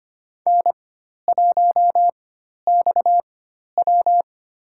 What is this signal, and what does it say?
Morse code, heard as a keyed tone, about 25 wpm, spaced as for 12 wpm, 700 Hz. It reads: N1XW
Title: USA Call Signs - 4 Characters - 4x Character Spacing - 25wpm